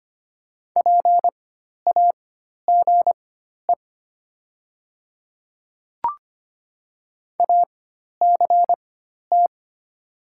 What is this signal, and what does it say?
Morse code, heard as a keyed tone, about 25 wpm, spaced as for 12 wpm, 700 Hz. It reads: PAGE ACT
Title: 4x Character Spacing - Top 500 Words - 25wpm